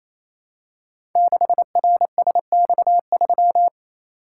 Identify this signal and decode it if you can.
Morse code, heard as a keyed tone, about 28 wpm, 700 Hz. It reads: 6RSX3
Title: Code Groups - 5 Letter:Numbers - 28wpm